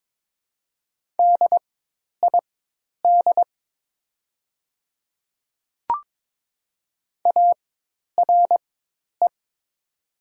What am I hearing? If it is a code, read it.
Morse code, heard as a keyed tone, about 22 wpm, spaced as for 10 wpm, 700 Hz. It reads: DID ARE
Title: 4x Character Spacing - Top 100 Words - 22wpm